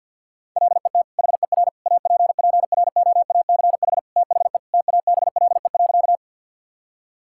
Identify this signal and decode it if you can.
Morse code, heard as a keyed tone, about 50 wpm, 700 Hz. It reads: LET HER APPROACH THE TABLE.
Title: Sentences from Top 1000 Words - Review Part 6 - 50wpm